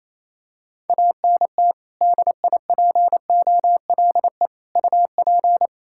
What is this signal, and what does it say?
Morse code, heard as a keyed tone, about 28 wpm, 700 Hz. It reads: ANT DIPOLE UP
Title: Practice QSO - Exchange 2 - Info - 28wpm